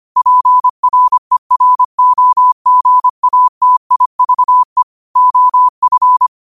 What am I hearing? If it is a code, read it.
Morse code, heard as a keyed tone, about 25 wpm, 1000 Hz. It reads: PREROGATIVE OF
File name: qotdfiles / 25wpm